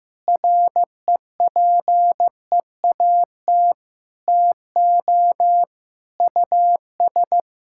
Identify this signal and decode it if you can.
Morse code, heard as a keyed tone, about 15 wpm, 700 Hz. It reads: REPEAT TO US
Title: Sentences from Top 1600 Words 15wpm